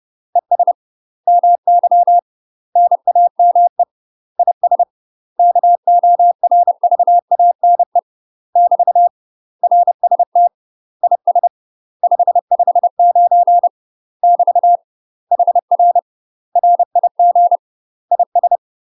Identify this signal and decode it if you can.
Morse code, heard as a keyed tone, about 30 wpm, 700 Hz. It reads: ES MY NAME IS KORVANE <BT> RST IS 559 <BT> HR RIG IS
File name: UK MASTER QSOs TRAINING / 30WPM / UK MASTER QSOs 0014